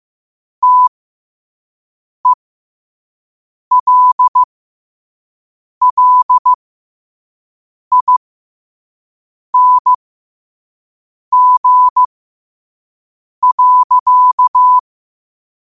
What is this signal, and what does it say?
Morse code, heard as a keyed tone, about 15 wpm, spaced as for 5 wpm, 1000 Hz. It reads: TELLING.
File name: qotdfiles / 5wpm